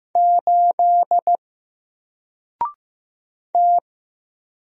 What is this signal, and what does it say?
Morse code, heard as a keyed tone, about 15 wpm, 700 Hz. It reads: 8 T
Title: Single Letter-Number Alt 15wpm